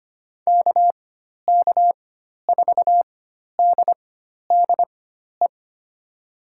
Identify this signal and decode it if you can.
Morse code, heard as a keyed tone, about 25 wpm, spaced as for 12 wpm, 700 Hz. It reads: KK4DDE
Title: USA Call Signs - 6 Characters - 4x Character Spacing - 25wpm